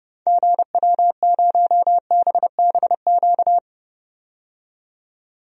Morse code, heard as a keyed tone, about 31 wpm, 700 Hz. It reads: GW0BBQ
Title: DXCC-100 Call Signs - 6 Characters - 30wpm